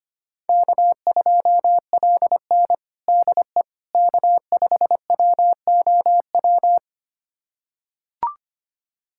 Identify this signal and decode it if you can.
Morse code, heard as a keyed tone, about 25 wpm, 700 Hz. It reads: K2LN DE K5WOW
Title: USA Callsign DE Callsign 25wpm